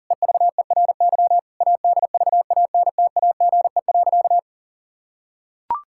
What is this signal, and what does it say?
Morse code, heard as a keyed tone, about 40 wpm, 700 Hz. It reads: EVERY ADVANTAGE.
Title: Sentences from Top 1000 Words - Review Part 6 - 40wpm